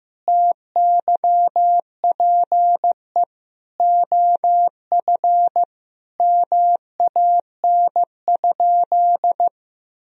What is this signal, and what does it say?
Morse code, heard as a keyed tone, about 15 wpm, 700 Hz. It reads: TYPE OF MAN?
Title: Sentences from Top 1000 Words - Review Part 3 - 15wpm